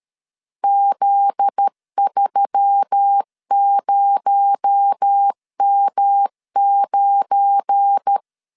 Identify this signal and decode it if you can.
Morse code, heard as a keyed tone, about 13 wpm, 800 Hz. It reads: Z30M9
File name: CW / DET / 1 9 1500C S B 12wpm 3819